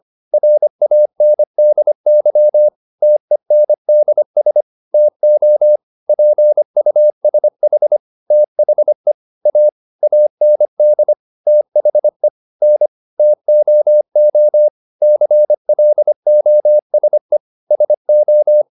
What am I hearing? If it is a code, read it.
Morse code, heard as a keyed tone, about 25 wpm, 600 Hz. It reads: RANDY TENDS TO PUSH THE A AND THE N TOO CLOSE SO